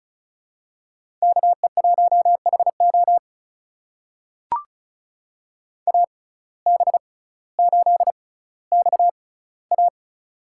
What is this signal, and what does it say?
Morse code, heard as a keyed tone, about 36 wpm, spaced as for 18 wpm, 700 Hz. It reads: KE1HO AB8XA
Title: USA Call Signs - 5 Characters - 6x Character Spacing - 35wpm